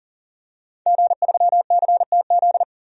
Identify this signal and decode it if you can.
Morse code, heard as a keyed tone, about 38 wpm, 700 Hz. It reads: G3CTZ
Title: DXCC-100 Call Signs - 5 Characters - 40wpm